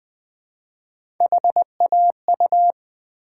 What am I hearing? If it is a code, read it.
Morse code, heard as a keyed tone, about 20 wpm, 700 Hz. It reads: HAU
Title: Build A Word - 1 Letter At A Time - 5-Letter Words - 20wpm